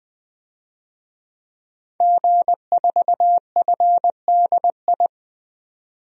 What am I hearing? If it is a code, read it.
Morse code, heard as a keyed tone, about 20 wpm, 700 Hz. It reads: G4FDI
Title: DXCC-100 Call Signs - 5 Characters - 20wpm